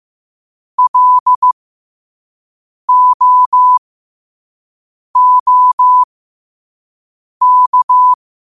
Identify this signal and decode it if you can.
Morse code, heard as a keyed tone, about 15 wpm, spaced as for 5 wpm, 1000 Hz. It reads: LOOK
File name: qotdfiles / 5wpm